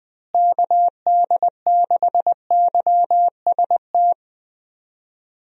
Morse code, heard as a keyed tone, about 20 wpm, 700 Hz. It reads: KD6YST
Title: USA Call Signs - 6 Characters - Repeated 3x - 20wpm